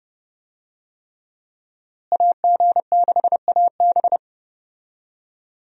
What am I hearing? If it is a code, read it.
Morse code, heard as a keyed tone, about 30 wpm, 700 Hz. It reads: AG6AB
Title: USA Call Signs - 5 Characters - Repeated 3x - 30wpm